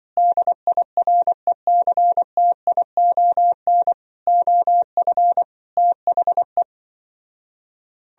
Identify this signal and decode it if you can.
Morse code, heard as a keyed tone, about 24 wpm, 700 Hz. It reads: DIRECTION OF THE
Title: Speed-Racing - Sets of 5 Words 20wpm